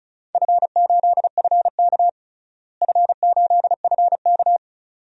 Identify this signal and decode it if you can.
Morse code, heard as a keyed tone, about 36 wpm, 700 Hz. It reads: F8FK F8FK
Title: DXCC-100 Call Signs - 4 Characters - Repeated 2x - 35wpm